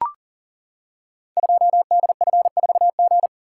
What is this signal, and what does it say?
Morse code, heard as a keyed tone, about 38 wpm, 700 Hz. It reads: 2DF4G
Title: Code Groups - 5 Letter:Numbers - 40wpm